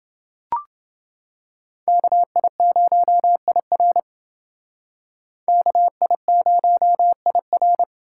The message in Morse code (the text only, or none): KI0IR KI0IR